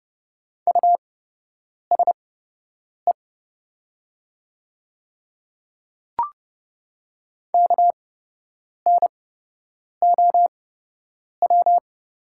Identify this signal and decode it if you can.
Morse code, heard as a keyed tone, about 29 wpm, spaced as for 8 wpm, 700 Hz. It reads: USE KNOW
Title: 8x Character Spacing - Top 100 Words - 30wpm